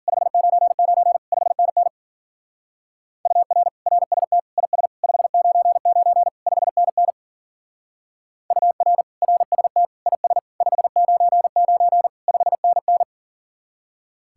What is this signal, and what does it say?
Morse code, heard as a keyed tone, about 47 wpm, 700 Hz. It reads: 599 5NN UR RST IS 599 5NN UR RST IS 599 5NN
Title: Speed-Racing Common QSO Phrases - 40wpm